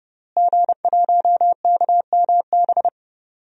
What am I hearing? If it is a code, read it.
Morse code, heard as a keyed tone, about 31 wpm, 700 Hz. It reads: G1KMB
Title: DXCC-100 Call Signs - 5 Characters - Repeated 2x - 30wpm